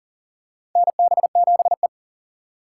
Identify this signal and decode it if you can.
Morse code, heard as a keyed tone, about 42 wpm, 700 Hz. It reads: NB7E
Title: USA Call Signs - 4 Characters - 3x Character Spacing - 40wpm